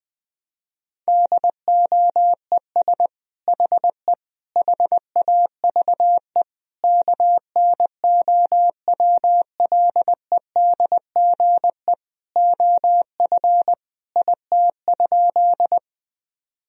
Morse code, heard as a keyed tone, about 20 wpm, 700 Hz. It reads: DOES HE HAVE KNOWLEDGE OF IT?
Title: Sentences from Top 1000 Words - Review Part 3 - 20wpm